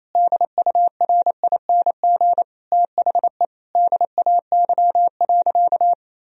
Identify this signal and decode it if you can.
Morse code, heard as a keyed tone, about 28 wpm, 700 Hz. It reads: DURING THE DAY.
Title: Sentences from Top 1000 Words - Review Part 2 - 28wpm